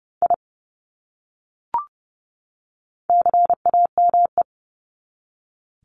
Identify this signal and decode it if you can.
Morse code, heard as a keyed tone, about 30 wpm, 700 Hz. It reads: I CAME
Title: Top 300 Words 30wpm